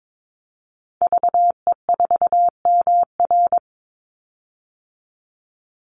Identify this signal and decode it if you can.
Morse code, heard as a keyed tone, about 22 wpm, 700 Hz. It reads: VE4MR
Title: Call Signs Repeated 2x - 22wpm